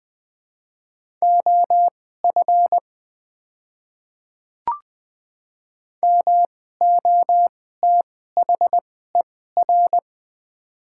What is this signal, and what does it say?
Morse code, heard as a keyed tone, about 20 wpm, spaced as for 14 wpm, 700 Hz. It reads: OF MOTHER
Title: 2x Character Spacing - Top 200 Words - 20wpm